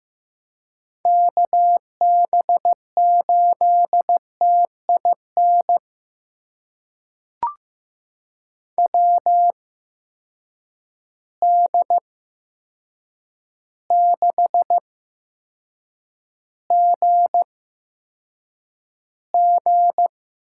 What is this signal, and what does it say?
Morse code, heard as a keyed tone, about 15 wpm, spaced as for 7 wpm, 700 Hz. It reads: KB8TIN WD6GG
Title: USA Call Signs - 6 Characters - 8x Character Spacing - 15wpm